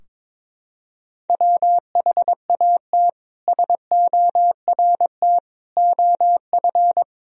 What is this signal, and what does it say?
Morse code, heard as a keyed tone, about 21 wpm, 700 Hz. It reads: WHAT SORT OF
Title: Sentences from Top 1000 Words - Review Part 3 - 22wpm